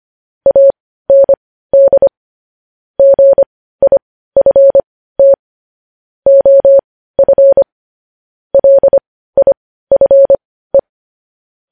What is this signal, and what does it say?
Morse code, heard as a keyed tone, about 25 wpm, spaced as for 15 wpm, 550 Hz. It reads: AND GIFT OF LIFE
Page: news-headlines-in-morse-code-at-15-wpm